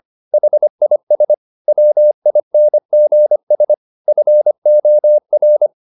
Morse code, heard as a keyed tone, about 25 wpm, 600 Hz. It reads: HIS WINGS FOR
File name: the old man and the sea part 1 25wpm